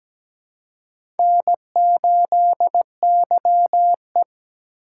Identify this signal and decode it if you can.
Morse code, heard as a keyed tone, about 17 wpm, 700 Hz. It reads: N8YE
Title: USA Call Signs - 4 Characters - 4x Character Spacing - 17wpm